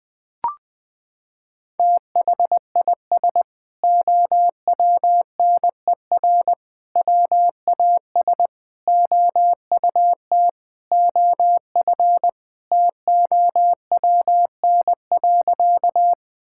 Morse code, heard as a keyed tone, about 21 wpm, 700 Hz. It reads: THIS OWNER WAS OUT OF TOWN.
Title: Sentences from Top 1600 Words 20wpm